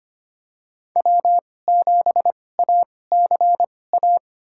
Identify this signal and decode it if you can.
Morse code, heard as a keyed tone, about 25 wpm, spaced as for 18 wpm, 700 Hz. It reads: W7ACA
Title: USA Call Signs - 5 Characters - 2x Character Spacing - 25wpm